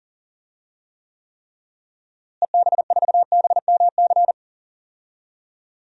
Morse code, heard as a keyed tone, about 40 wpm, 700 Hz. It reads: EB4BMC
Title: DXCC-100 Call Signs - 6 Characters - Repeated 3x - 40wpm